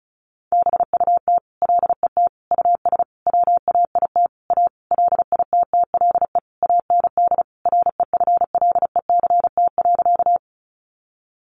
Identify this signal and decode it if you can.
Morse code, heard as a keyed tone, about 34 wpm, 700 Hz. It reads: BUT LET US WAIT A LITTLE AND REFLECT.
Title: Sentences from Top 1500 Words 35wpm